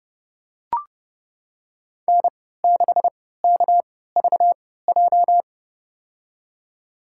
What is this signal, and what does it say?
Morse code, heard as a keyed tone, about 30 wpm, spaced as for 17 wpm, 700 Hz. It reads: N6KVJ
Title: USA Call Signs - 3x Character Spacing - 30wpm